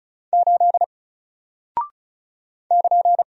Morse code, heard as a keyed tone, about 36 wpm, 700 Hz. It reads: 8 <KN>
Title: Single Character (Letter:Number:Punctuation:Prosign) Alt 35wpm